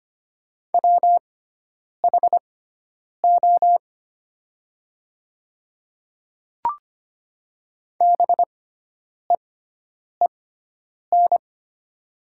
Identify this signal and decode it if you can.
Morse code, heard as a keyed tone, about 25 wpm, spaced as for 9 wpm, 700 Hz. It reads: WHO BEEN
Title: 6x Character Spacing - Top 100 Words - 25wpm